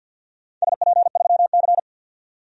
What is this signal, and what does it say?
Morse code, heard as a keyed tone, about 51 wpm, 700 Hz. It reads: SP3C
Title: DXCC-100 Call Signs - 4 Characters - 50wpm